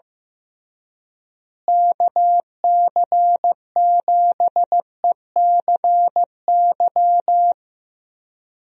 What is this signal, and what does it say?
Morse code, heard as a keyed tone, about 15 wpm, 700 Hz. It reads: KC7ECY
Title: USA Call Signs - 6 Characters - Repeated 3x - 15wpm